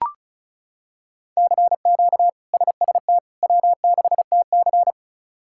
Morse code, heard as a keyed tone, about 36 wpm, 700 Hz. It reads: CQ SST W6TC
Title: K1USN SST QSO Practice - 35wpm